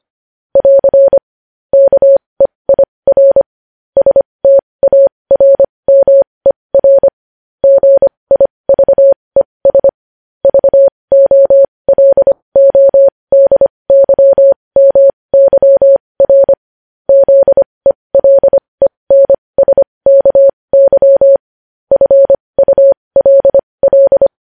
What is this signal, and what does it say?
Morse code, heard as a keyed tone, about 25 wpm, spaced as for 20 wpm, 550 Hz. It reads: <AR> KEIR STARMER GIVES VOLODYMYR ZELENSKY FULL
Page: news-headlines-in-morse-code-at-20-wpm